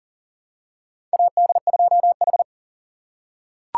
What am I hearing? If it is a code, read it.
Morse code, heard as a keyed tone, about 40 wpm, 700 Hz. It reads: AD2H
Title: USA Call Signs - 4 Characters - 4x Character Spacing - 40wpm